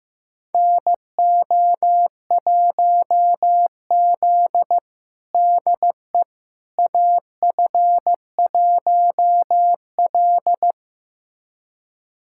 Morse code, heard as a keyed tone, about 16 wpm, 700 Hz. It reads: NO1Z DE AF1L
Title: USA Callsign DE Callsign - 4 Characters - 15wpm